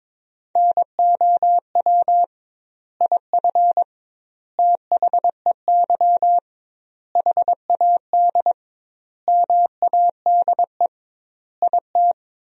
Morse code, heard as a keyed tone, about 22 wpm, 700 Hz. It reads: NOW IF THEY HAD MADE IT
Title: 2x Word-Spacing - Sentences from Top 500 Words - 22wpm